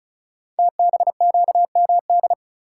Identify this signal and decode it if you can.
Morse code, heard as a keyed tone, about 36 wpm, 700 Hz. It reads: TBQMD
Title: Code Groups - 5 Letters - 35wpm